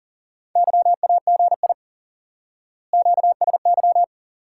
Morse code, heard as a keyed tone, about 40 wpm, 700 Hz. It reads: YAGI QSY
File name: course / instant-qso-element-course-3 / IQE Course 3 - Lesson 89 - YAGI - Two Elements - 40wpm